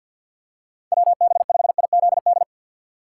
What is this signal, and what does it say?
Morse code, heard as a keyed tone, about 52 wpm, 690 Hz. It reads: WB5IZD
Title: USA Call Signs - 2x Character Spacing - 50wpm